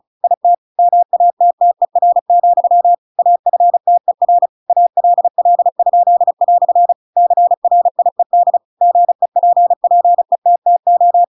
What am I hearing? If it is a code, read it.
Morse code, heard as a keyed tone, about 34 wpm, 700 Hz. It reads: IT MATTER, AFTER ALL?" CRIED GEPPETTO
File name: Pinocchio / Pinocchio ch8 35 35 ik1hge